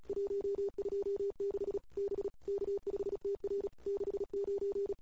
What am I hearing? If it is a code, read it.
Morse code, heard as a keyed tone, about 35 wpm, 410 Hz. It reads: 126 B K5TR 69
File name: CW / 2008-ss-cw-k5tr-op-n5rz -22:00